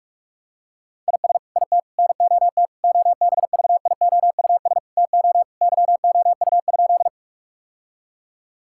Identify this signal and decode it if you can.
Morse code, heard as a keyed tone, about 46 wpm, 700 Hz. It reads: IS IT NOT OBVIOUS TO YOU?
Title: Sentences from Top 1600 Words 45wpm